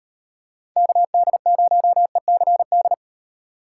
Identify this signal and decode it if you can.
Morse code, heard as a keyed tone, about 38 wpm, 700 Hz. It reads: KD0ECD